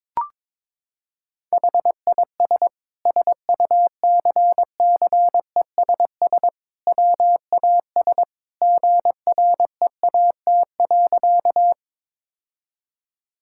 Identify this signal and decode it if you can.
Morse code, heard as a keyed tone, about 22 wpm, 700 Hz. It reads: HIS SUCCESS WAS GREAT.
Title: Sentences from Top 1000 Words - Review Part 6 - 22wpm